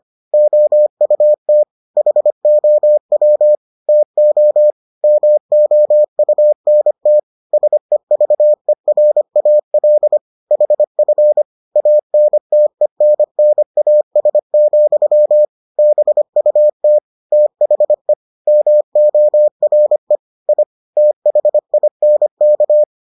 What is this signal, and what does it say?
Morse code, heard as a keyed tone, about 25 wpm, 600 Hz. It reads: OUT HOW TO MOUNT SEVERAL HF ANTENNAS, BUT THE MORE I THINK